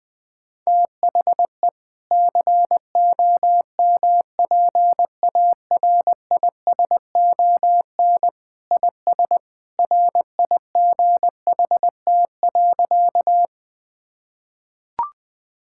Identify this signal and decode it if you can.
Morse code, heard as a keyed tone, about 20 wpm, 700 Hz. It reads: THE COMPARISON IS RIGHT.
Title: Sentences from Top 1600 Words 20wpm